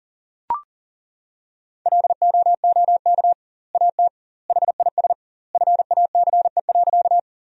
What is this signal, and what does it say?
Morse code, heard as a keyed tone, about 40 wpm, 700 Hz. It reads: LOOK AT HIS FACE.
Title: 2x Word-Spacing - Sentences from Top 300 Words - 40wpm